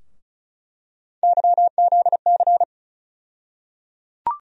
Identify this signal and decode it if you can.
Morse code, heard as a keyed tone, about 36 wpm, 700 Hz. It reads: YZC
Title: USA Call Sign Postfixes - 3 Letters 35wpm